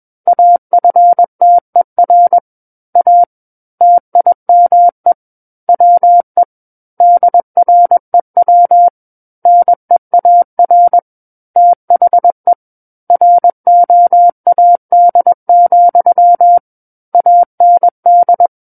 Morse code, heard as a keyed tone, about 21 wpm, 700 Hz. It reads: AFTER A TIME WE DREW NEAR THE ROAD, AND
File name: Book One / 12-Chapter 12 - 21 WPM